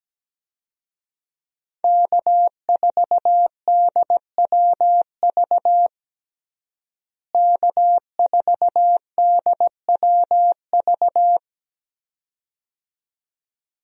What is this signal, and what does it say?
Morse code, heard as a keyed tone, about 17 wpm, 700 Hz. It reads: K4DWV K4DWV
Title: USA Call Signs - 5 Characters - Repeated 2x - 17wpm